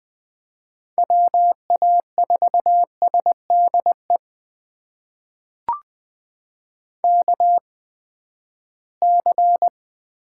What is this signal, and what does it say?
Morse code, heard as a keyed tone, about 20 wpm, spaced as for 14 wpm, 700 Hz. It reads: WA4SDE KC
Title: USA Call Signs - 6 Characters - 8x Character Spacing - 20wpm